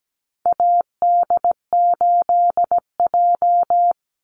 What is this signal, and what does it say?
Morse code, heard as a keyed tone, about 17 wpm, 700 Hz. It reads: AD8J
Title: Call Signs - 17wpm